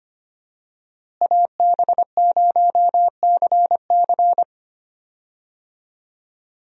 Morse code, heard as a keyed tone, about 25 wpm, 700 Hz. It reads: AB0CC
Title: USA Call Signs - 5 Characters - Repeated 2x - 25wpm